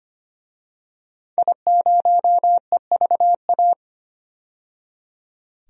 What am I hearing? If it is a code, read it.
Morse code, heard as a keyed tone, about 25 wpm, 700 Hz. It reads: I0EVA
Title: DXCC-100 Call Signs - 5 Characters - 25wpm